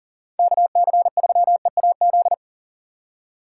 Morse code, heard as a keyed tone, about 40 wpm, 700 Hz. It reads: KC3EAZ
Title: USA Call Signs - 6 Characters - 2x Character Spacing - 40wpm